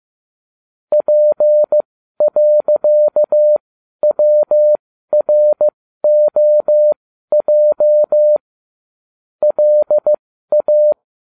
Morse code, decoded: P.WROJ LA

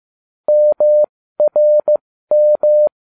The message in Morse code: MRM

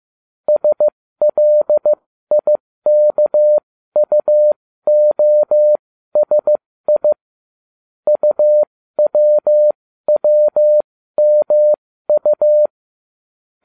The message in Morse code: SLIKUOSI UWWMU

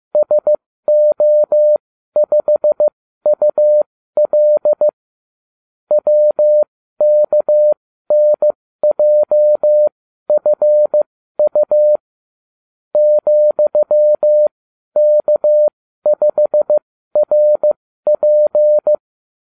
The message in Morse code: SO5UL WKNJFU ,K5RP

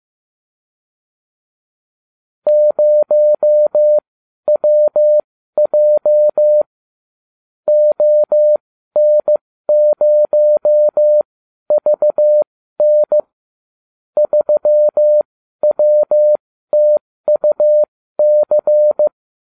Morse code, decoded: 0WJ ON0VN 3WTUC

